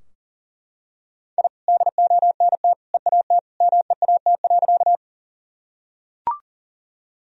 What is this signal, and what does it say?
Morse code, heard as a keyed tone, about 40 wpm, 700 Hz. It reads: I DONT EAT MEAT.
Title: Sentences from Top 1000 Words - Review Part 3 - 40wpm